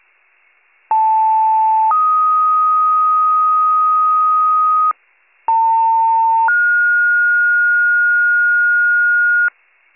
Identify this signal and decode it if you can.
Two-tone page, heard as a one-second tone, then a three-second tone, about 880 then 1250 Hz, about 900 then 1430 Hz, with noise. It is heard from 0.9 to 9.5 s.